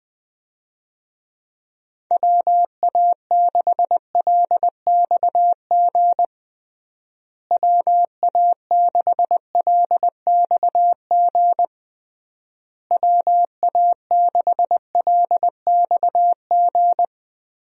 Morse code, heard as a keyed tone, about 21 wpm, 700 Hz. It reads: WA6LXG WA6LXG WA6LXG
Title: USA Call Signs - 6 Characters - Repeated 3x - 20wpm